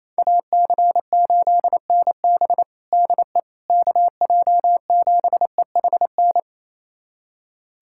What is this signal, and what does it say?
Morse code, heard as a keyed tone, about 28 wpm, 700 Hz. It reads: AC8NB DE KJ7EHN
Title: USA Callsign DE Callsign 28wpm